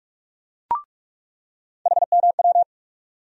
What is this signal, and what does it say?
Morse code, heard as a keyed tone, about 42 wpm, 700 Hz. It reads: HMW
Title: USA Call Sign Postfixes - 3 Letters 45wpm